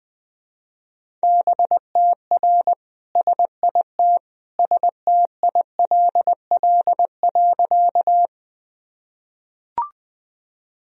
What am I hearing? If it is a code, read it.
Morse code, heard as a keyed tone, about 20 wpm, 700 Hz. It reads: BTR SIT STILL.
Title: Sentences from Top 500 Words and Abbreviations 20wpm